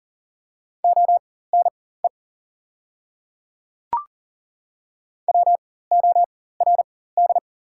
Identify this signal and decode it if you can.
Morse code, heard as a keyed tone, about 37 wpm, spaced as for 18 wpm, 690 Hz. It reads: ONE WORD
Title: 4x Character Spacing - Top 100 Words - 40wpm